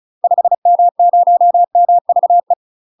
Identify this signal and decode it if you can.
Morse code, heard as a keyed tone, about 34 wpm, 700 Hz. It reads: 5M0MVE
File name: UK MASTER QSOs TRAINING / 35WPM / UK MASTER QSOs 0019